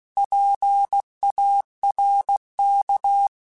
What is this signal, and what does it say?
Morse code, heard as a keyed tone, about 16 wpm, 800 Hz. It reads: PARK
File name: fromSMs / FromSMs mp3s / Cities@15